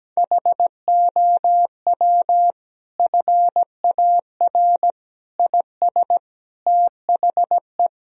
Morse code, heard as a keyed tone, about 17 wpm, 700 Hz. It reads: HOW FAR IS THE
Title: Sentences from Top 1000 Words - Review Part 4 - 17wpm